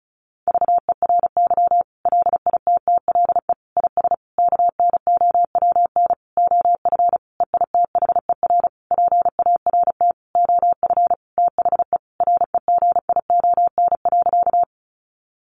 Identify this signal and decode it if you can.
Morse code, heard as a keyed tone, about 34 wpm, 700 Hz. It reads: VERY LITTLE IS KNOWN OF EITHER PART OF THE REGION.
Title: Sentences from Top 1500 Words 35wpm